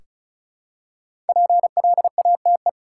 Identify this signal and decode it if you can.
Morse code, heard as a keyed tone, about 36 wpm, 700 Hz. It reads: PLATE
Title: Five Letter Words - 35wpm